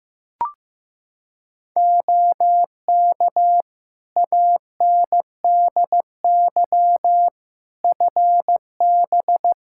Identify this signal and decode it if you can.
Morse code, heard as a keyed tone, about 15 wpm, 700 Hz. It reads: OK ANDY FB
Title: K1USN SST QSO Practice 15wpm